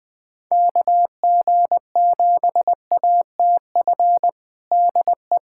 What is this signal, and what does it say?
Morse code, heard as a keyed tone, about 20 wpm, 700 Hz. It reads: KG7ATF DE
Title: USA Callsign DE Callsign - 6 Characters - 20wpm